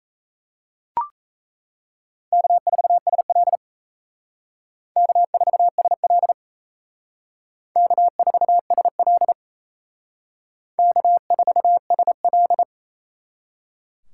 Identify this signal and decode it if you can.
Morse code, heard as a keyed tone, about 37 wpm, 700 Hz. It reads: K4SL K4SL K4SL K4SL